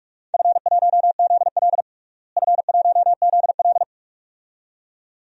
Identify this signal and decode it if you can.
Morse code, heard as a keyed tone, about 46 wpm, 700 Hz. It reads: F1ZL F1ZL
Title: DXCC-100 Call Signs - 4 Characters - Repeated 2x - 45wpm